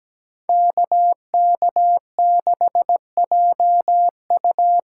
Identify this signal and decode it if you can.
Morse code, heard as a keyed tone, about 17 wpm, 700 Hz. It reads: KK6JU
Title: USA Call Signs - 6 Characters - 8x Character Spacing - 17wpm